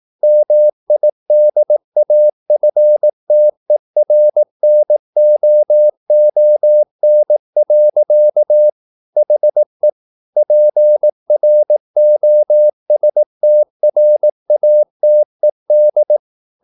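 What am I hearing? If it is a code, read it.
Morse code, heard as a keyed tone, about 18 wpm, 600 Hz. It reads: MIDAFTERNOON. HE PROSTRATED
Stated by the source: Morse practice at 18 wpm, English text, from CW med Gnister.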